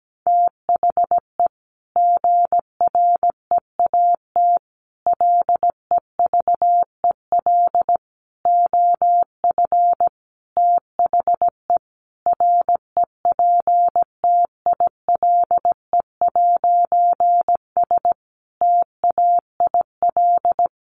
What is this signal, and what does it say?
Morse code, heard as a keyed tone, about 17 wpm, 700 Hz. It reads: THE GREAT LEVEL OF THE REPTILE'S TAIL
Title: A Sound Of Thunder 17wpm